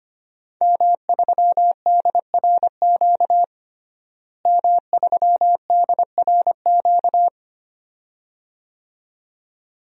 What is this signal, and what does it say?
Morse code, heard as a keyed tone, about 25 wpm, 700 Hz. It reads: M3DRQ M3DRQ
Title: DXCC-100 Call Signs - 5 Characters - Repeated 2x - 25wpm